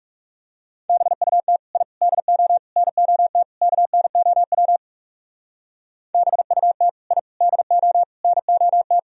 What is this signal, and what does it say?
Morse code, heard as a keyed tone, about 38 wpm, 700 Hz. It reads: BUT I DO NOT KNOW BUT I DO NOT